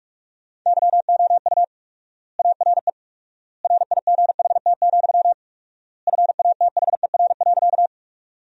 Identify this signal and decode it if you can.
Morse code, heard as a keyed tone, about 46 wpm, 700 Hz. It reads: YOU ARE RIGHT, FATHER.
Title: 4x Word-Spacing - Sentences from Top 300 Words - 45wpm